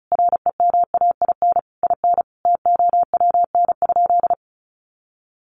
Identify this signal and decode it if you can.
Morse code, heard as a keyed tone, about 34 wpm, 700 Hz. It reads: REMAIN IN TOWN?
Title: Sentences from Top 1200 Words 35wpm